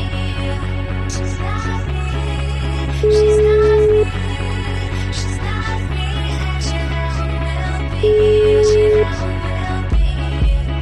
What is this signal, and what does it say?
Ringback tone (Europe, 425 Hz), with noise.